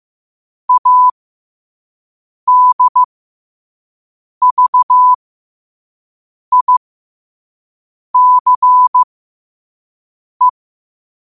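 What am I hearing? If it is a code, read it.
Morse code, heard as a keyed tone, about 15 wpm, spaced as for 5 wpm, 1000 Hz. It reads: ADVICE
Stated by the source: Morse code practice at 5 wpm.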